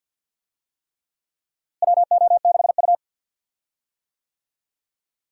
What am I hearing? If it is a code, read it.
Morse code, heard as a keyed tone, about 48 wpm, 700 Hz. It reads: WO6U
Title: USA Call Signs - 4 Characters - Repeated 3x - 50wpm